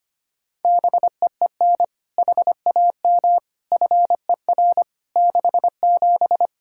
Morse code, heard as a keyed tone, about 25 wpm, 700 Hz. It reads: BEEN HAM FER 67
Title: Practice QSO - Exchange 3 - More Info - 25wpm